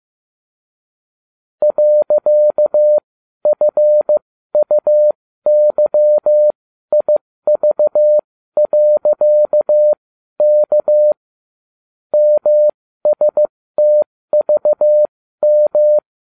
Morse code, heard as a keyed tone, about 15 wpm, spaced as for 12 wpm, 600 Hz. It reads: .FUYIV.K MSTVM